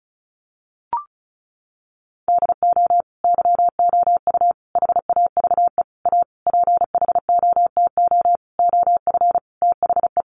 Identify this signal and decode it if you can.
Morse code, heard as a keyed tone, about 34 wpm, 700 Hz. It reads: DO YOU HAVE A PHOTO OF THE
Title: Sentences from Top 1200 Words 35wpm